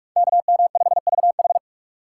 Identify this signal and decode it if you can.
Morse code, heard as a keyed tone, about 42 wpm, 700 Hz. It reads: KM5VH
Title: USA Call Signs - 5 Characters - 6x Character Spacing - 45wpm